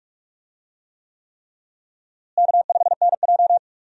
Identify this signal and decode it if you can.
Morse code, heard as a keyed tone, about 43 wpm, 700 Hz. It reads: K5NJ
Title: USA Call Signs - 4 Characters - Repeated 3x - 45wpm